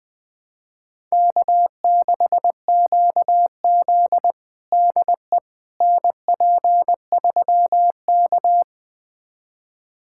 Morse code, heard as a keyed tone, about 20 wpm, 700 Hz. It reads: K6QZ DE NP3K
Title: USA Callsign DE Callsign - 4 Characters - 20wpm